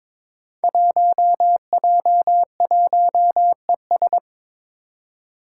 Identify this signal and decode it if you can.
Morse code, heard as a keyed tone, about 22 wpm, 700 Hz. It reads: 1J1ES
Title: Five Challenging Letter-Number Combinations 22wpm